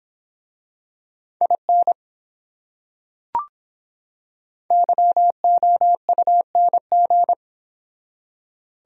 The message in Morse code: IN YOUNG

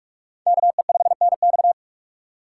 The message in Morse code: KE5NX

